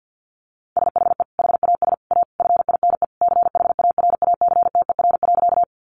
Morse code, heard as a keyed tone, about 50 wpm, 700 Hz. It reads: SHE HAS A FINE CHARACTER.
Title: Sentences from Top 1200 Words 50wpm